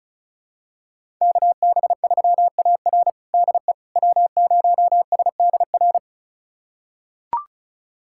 Morse code, heard as a keyed tone, about 36 wpm, 700 Hz. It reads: KB3AR DE W0SDR
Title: USA Callsign DE Callsign - 5 Characters - 35wpm